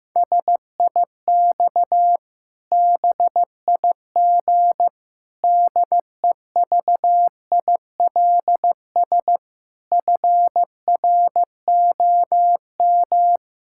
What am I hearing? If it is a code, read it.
Morse code, heard as a keyed tone, about 15 wpm, 700 Hz. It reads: SIX BIG DEVILS FROM
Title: Pangram Challenge v2 - 15wpm